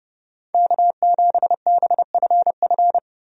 Morse code, heard as a keyed tone, about 32 wpm, 700 Hz. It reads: K7BFF
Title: USA Call Signs - 5 Characters - 30wpm